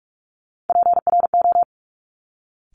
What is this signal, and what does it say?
Morse code, heard as a keyed tone, about 46 wpm, 700 Hz. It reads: PRO
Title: Top Three Letter Combinations in Words 45wpm